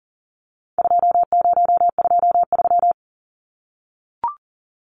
Morse code, heard as a keyed tone, about 40 wpm, 700 Hz. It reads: 2023
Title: Numbers - 4-Digit 40wpm